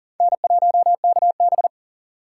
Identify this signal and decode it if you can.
Morse code, heard as a keyed tone, about 50 wpm, 700 Hz. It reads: N1KB
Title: Speed Racing - USA 1x2 Call Signs 40wpm